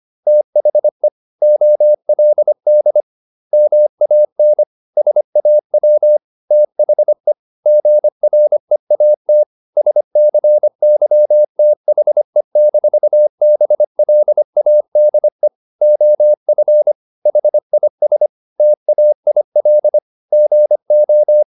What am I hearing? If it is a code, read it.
Morse code, heard as a keyed tone, about 25 wpm, 600 Hz. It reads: THE OLD MAN SAW THE GREAT SCYTHE-BLADE OF HIS TAIL GO
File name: the old man and the sea part 1 25wpm